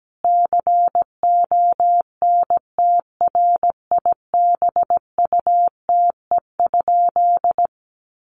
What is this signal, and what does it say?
Morse code, heard as a keyed tone, about 17 wpm, 700 Hz. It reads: CONTRIBUTE?
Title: Sentences from Top 1400 Words 17wpm